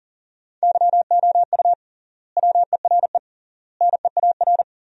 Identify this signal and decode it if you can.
Morse code, heard as a keyed tone, about 37 wpm, 700 Hz. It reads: YOU WERE NEAR
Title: 3x Word-Spacing - Sentences from Top 200 Words - 40wpm